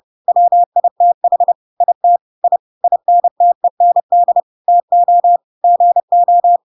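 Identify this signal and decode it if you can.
Morse code, heard as a keyed tone, about 30 wpm, 700 Hz. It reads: WITH IT I INTEND TO GO
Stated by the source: Chapter 2 of The Adventures of Pinocchio read as a book in Morse code (CW) at 30 wpm, by ik1hge.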